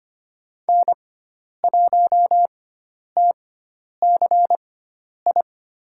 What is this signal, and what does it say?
Morse code, heard as a keyed tone, about 25 wpm, spaced as for 10 wpm, 700 Hz. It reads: N1TCI
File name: Lesson 16g - Letter c - Call Signs 25wpm